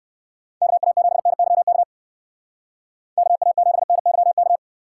Morse code, heard as a keyed tone, about 59 wpm, 700 Hz. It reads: KA7NYK KA7NYK